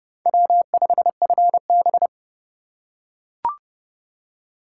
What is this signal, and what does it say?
Morse code, heard as a keyed tone, about 29 wpm, 700 Hz. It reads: W5FB